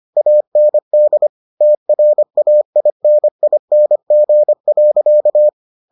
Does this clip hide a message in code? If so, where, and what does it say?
Yes: Morse code from 0.2 to 5.5 s, heard as a keyed tone, about 25 wpm, 600 Hz. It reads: AND TRAINING.